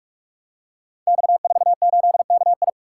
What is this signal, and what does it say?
Morse code, heard as a keyed tone, about 44 wpm, 700 Hz. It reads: X48KI
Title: Code Groups - 5 Letter:Numbers - 45wpm